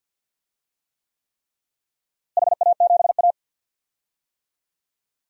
Morse code, heard as a keyed tone, about 53 wpm, 700 Hz. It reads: HA7A
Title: DXCC-100 Call Signs - 4 Characters - Repeated 3x - 50wpm